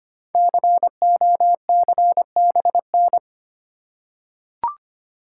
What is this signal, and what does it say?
Morse code, heard as a keyed tone, about 25 wpm, 700 Hz. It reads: COCBN
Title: Code Groups - 5 Letters - 25wpm